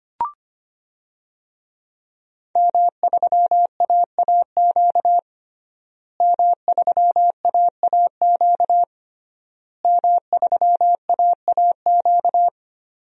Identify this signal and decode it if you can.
Morse code, heard as a keyed tone, about 25 wpm, 700 Hz. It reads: M3AAQ M3AAQ M3AAQ
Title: DXCC-100 Call Signs - 5 Characters - Repeated 3x - 25wpm